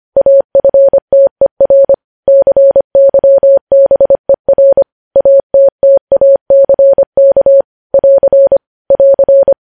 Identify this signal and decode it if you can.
Morse code, heard as a keyed tone, about 25 wpm, 550 Hz. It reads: AFTER CYBER ATTACK <AR> <AR>
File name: podcast / News headlines-1x-25x25